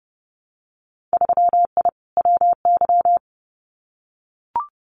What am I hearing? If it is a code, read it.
Morse code, heard as a keyed tone, about 31 wpm, 700 Hz. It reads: 3I WY
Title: Winter Field Day Exchanges Encore 30wpm